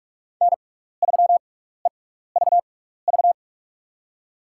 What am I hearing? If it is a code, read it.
Morse code, heard as a keyed tone, about 46 wpm, spaced as for 16 wpm, 700 Hz. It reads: N3EVV
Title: USA Call Signs - 5 Characters - 6x Character Spacing - 45wpm